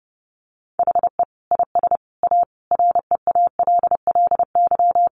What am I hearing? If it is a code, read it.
Morse code, heard as a keyed tone, about 30 wpm, 700 Hz. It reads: HE IS A REALLY
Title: Sentences from Top 1200 Words 30wpm